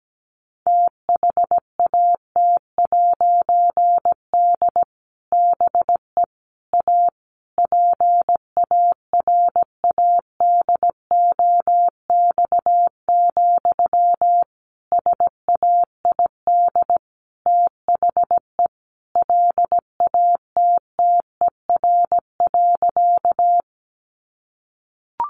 Morse code, heard as a keyed tone, about 17 wpm, 700 Hz. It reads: THAT'D BE A PARADOX, SAID THE LATTER.
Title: A Sound Of Thunder 17wpm